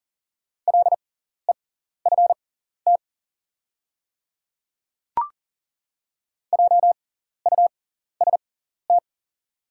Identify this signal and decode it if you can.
Morse code, heard as a keyed tone, about 40 wpm, spaced as for 14 wpm, 690 Hz. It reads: LEFT JUST
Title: 6x Character Spacing - Top 500 Words - 40wpm